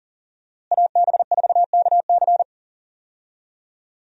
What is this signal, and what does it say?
Morse code, heard as a keyed tone, about 40 wpm, 700 Hz. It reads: AB4KC
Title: USA Call Signs - 5 Characters - Repeated 2x - 40wpm